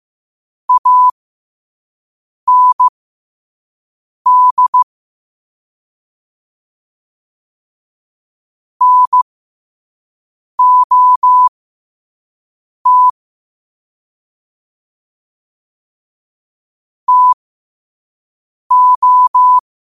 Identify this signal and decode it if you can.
Morse code, heard as a keyed tone, about 15 wpm, spaced as for 5 wpm, 1000 Hz. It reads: AND NOT TO
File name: qotdfiles / 5wpm